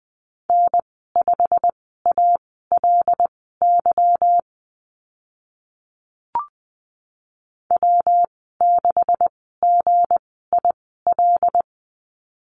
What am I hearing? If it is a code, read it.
Morse code, heard as a keyed tone, about 20 wpm, spaced as for 15 wpm, 700 Hz. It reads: N5ALY W6GIL
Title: 186 Lesson 30g - Letter b - Call Signs 20wpm